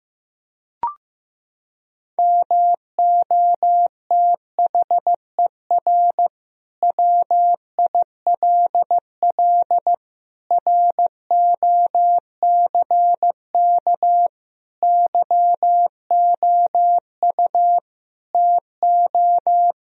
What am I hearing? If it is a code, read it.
Morse code, heard as a keyed tone, about 15 wpm, 700 Hz. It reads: MOTHER WILL ROCK YOU TO
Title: Sentences from Top 1000 Words - Review Part 2 - 15wpm